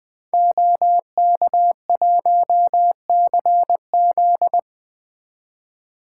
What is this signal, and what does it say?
Morse code, heard as a keyed tone, about 20 wpm, 700 Hz. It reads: OK1CZ
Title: 2023 SOTA Activator Call Signs - 20wpm